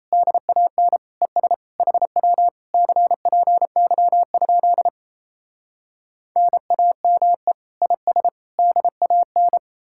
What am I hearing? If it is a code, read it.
Morse code, heard as a keyed tone, about 38 wpm, 700 Hz. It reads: DAN ES HW CPY? NAME IS DAN